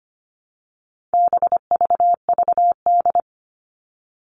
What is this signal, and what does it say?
Morse code, heard as a keyed tone, about 25 wpm, 700 Hz. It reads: BVVD
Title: All Cut Numbers - 4-Digit 25wpm